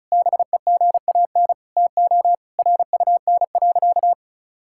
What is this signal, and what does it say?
Morse code, heard as a keyed tone, about 36 wpm, 690 Hz. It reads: BEGAN TO RUN.
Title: Sentences from Top 1600 Words 35wpm